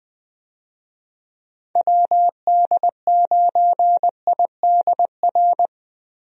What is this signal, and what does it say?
Morse code, heard as a keyed tone, about 20 wpm, 700 Hz. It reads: WD9IDR
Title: USA Call Signs - 6 Characters - Repeated 2x - 20wpm